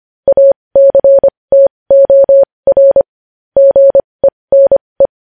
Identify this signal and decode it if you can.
Morse code, heard as a keyed tone, about 25 wpm, spaced as for 20 wpm, 550 Hz. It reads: ACTOR GENE
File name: podcast / News headlines-1x-25x20